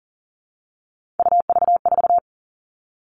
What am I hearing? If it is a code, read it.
Morse code, heard as a keyed tone, about 40 wpm, 700 Hz. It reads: UV4
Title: All Cut Numbers - 3-Digit 40wpm